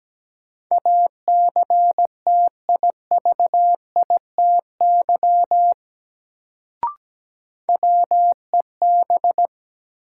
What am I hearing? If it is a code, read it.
Morse code, heard as a keyed tone, about 17 wpm, 700 Hz. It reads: ACTIVITY WEB
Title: Sentences from Top 1000 Words - Review Part 4 - New Word Drill - 17wpm